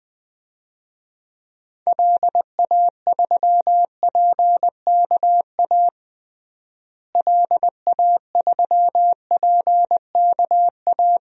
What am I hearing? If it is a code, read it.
Morse code, heard as a keyed tone, about 21 wpm, 700 Hz. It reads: LA3PKA LA3PKA
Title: DXCC-100 Call Signs - 6 Characters - Repeated 2x - 20wpm